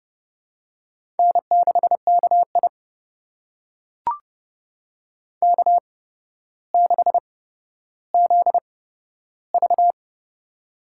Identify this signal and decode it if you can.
Morse code, heard as a keyed tone, about 29 wpm, spaced as for 13 wpm, 700 Hz. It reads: N6KI K6ZV
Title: USA Call Signs - 4 Characters - 8x Character Spacing - 30wpm